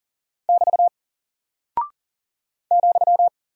Morse code, heard as a keyed tone, about 39 wpm, 700 Hz. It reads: <BT> ,
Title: Single Character (Letter:Number:Punctuation:Prosign) 40wpm